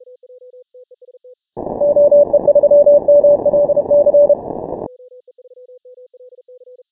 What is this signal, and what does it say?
Morse code, heard as a keyed tone, about 31 wpm, 580 Hz. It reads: OE3MLC
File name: final / W1UJ-168-140-MorseRunner